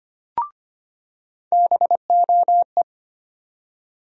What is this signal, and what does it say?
Morse code, heard as a keyed tone, about 25 wpm, 700 Hz. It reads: BOE
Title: USA Call Sign Postfixes - 3 Letters 25wpm